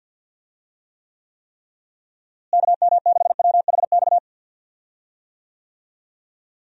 Morse code, heard as a keyed tone, about 50 wpm, 700 Hz. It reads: KM6WHX
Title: USA Call Signs - 6 Characters - Repeated 3x - 50wpm